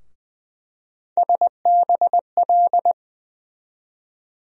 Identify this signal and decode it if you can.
Morse code, heard as a keyed tone, about 20 wpm, 700 Hz. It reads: SBL
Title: USA Call Sign Postfixes - 3 Letters 20wpm